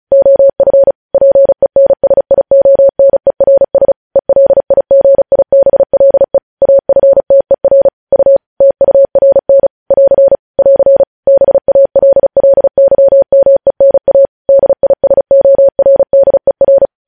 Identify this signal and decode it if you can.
Morse code, heard as a keyed tone, about 36 wpm, 550 Hz. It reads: OF PENSIONERS ELIGIBLE AFTER U TURN <AR> <AR> BALLYMENA DISORDER
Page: news headlines in morse code at 35 wpm